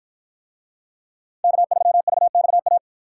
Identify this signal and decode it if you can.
Morse code, heard as a keyed tone, about 53 wpm, 700 Hz. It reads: K3VXA